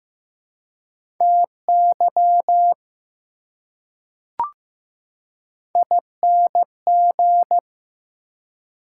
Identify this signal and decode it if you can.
Morse code, heard as a keyed tone, about 15 wpm, 700 Hz. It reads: TY ING
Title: Prefixes and Suffixes 15wpm